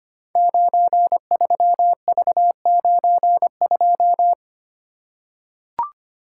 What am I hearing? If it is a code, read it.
Morse code, heard as a keyed tone, about 25 wpm, 700 Hz. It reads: 93V92
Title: Code Groups - 5 Letter:Numbers - 25wpm